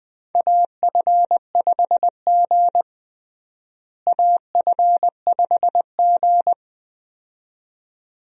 Morse code, heard as a keyed tone, about 21 wpm, 700 Hz. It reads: AF5G AF5G
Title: USA Call Signs - 4 Characters - Repeated 2x - 20wpm